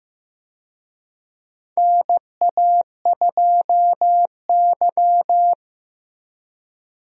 Morse code, heard as a keyed tone, about 15 wpm, 700 Hz. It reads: NA2Y
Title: USA Call Signs - 4 Characters - Repeated 3x - 15wpm